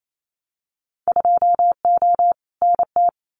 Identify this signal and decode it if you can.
Morse code, heard as a keyed tone, about 28 wpm, 700 Hz. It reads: 2O NT
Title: Winter Field Day Exchanges Encore 28wpm